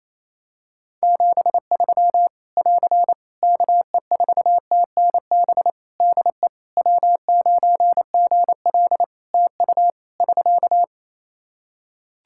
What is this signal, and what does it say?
Morse code, heard as a keyed tone, about 28 wpm, 700 Hz. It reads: 73 <AR> KE4TNB DE W9GL TU <SK>
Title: K1USN SST QSO Practice 28wpm